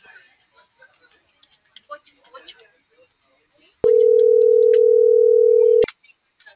Ringback tone (North America), 440 + 480 Hz.